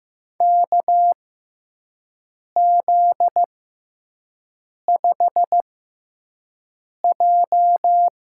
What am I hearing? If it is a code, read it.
Morse code, heard as a keyed tone, about 15 wpm, spaced as for 5 wpm, 700 Hz. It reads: KZ5J
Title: USA Call Signs - 6x Character Spacing - 15wpm